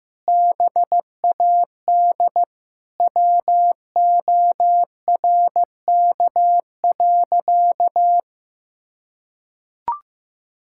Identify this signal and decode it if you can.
Morse code, heard as a keyed tone, about 15 wpm, 700 Hz. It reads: BAD WORK.
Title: Sentences from Top 1000 Words - Review Part 6 - 15wpm